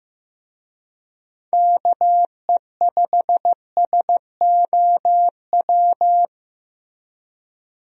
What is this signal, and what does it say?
Morse code, heard as a keyed tone, about 15 wpm, 700 Hz. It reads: KE5SOW
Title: USA Call Signs - 6 Characters - Repeated 3x - 15wpm